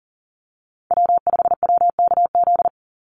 Morse code, heard as a keyed tone, about 40 wpm, 700 Hz. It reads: W5WKZ
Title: USA Call Signs 40wpm